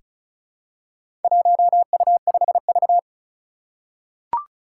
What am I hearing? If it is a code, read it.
Morse code, heard as a keyed tone, about 36 wpm, 700 Hz. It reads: 1U5V
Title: Four Challenging Letter-Number Combinations 35wpm